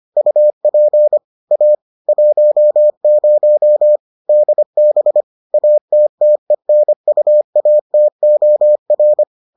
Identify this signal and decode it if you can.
Morse code, heard as a keyed tone, about 25 wpm, 600 Hz. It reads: UP A 10 DB ATTENUATOR